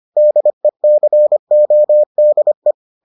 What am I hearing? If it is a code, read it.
Morse code, heard as a keyed tone, about 25 wpm, 600 Hz. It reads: DECODE